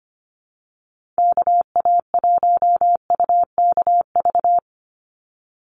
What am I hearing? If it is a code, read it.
Morse code, heard as a keyed tone, about 25 wpm, 700 Hz. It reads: KA1UKV
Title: USA Call Signs 25wpm